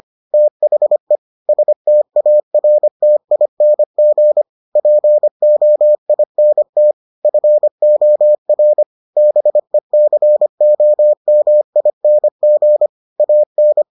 Morse code, heard as a keyed tone, about 25 wpm, 600 Hz. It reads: THE STARTING POINT FOR BECOMING AN